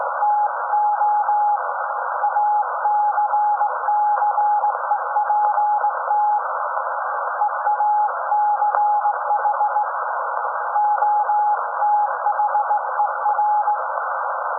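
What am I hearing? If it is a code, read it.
Morse code, heard as a keyed tone, about 18 wpm, 840 Hz. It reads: TO ACCEPT EACH OTHER
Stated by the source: SAQ Grimeton Radio, Alexanderson 200 kW alternator on 17.2 kHz, received in Potsdam with a very strong signal on World Radio Day.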